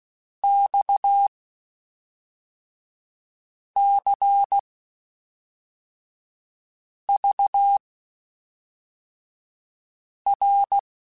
Morse code, heard as a keyed tone, about 16 wpm, spaced as for 3 wpm, 800 Hz. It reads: XCVR
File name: (spaced) / QSOs Spaced-mp3s / WD0FBT 1 @5 sp